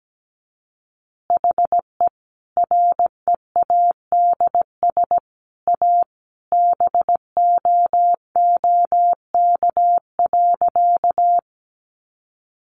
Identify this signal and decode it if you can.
Morse code, heard as a keyed tone, about 17 wpm, 700 Hz. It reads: HE READS A BOOK.
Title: Additional Easy Phrases 17wpm